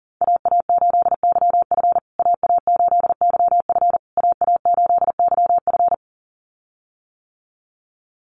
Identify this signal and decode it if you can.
Morse code, heard as a keyed tone, about 40 wpm, 700 Hz. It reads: AA8YF AA8YF AA8YF
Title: USA Call Signs Repeated 3x 40wpm